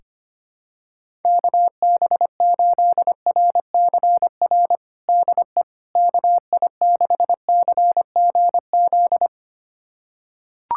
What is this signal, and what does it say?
Morse code, heard as a keyed tone, about 25 wpm, 700 Hz. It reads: KB8RCR DE KI6CGZ
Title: USA Callsign DE Callsign - 6 Characters - 25wpm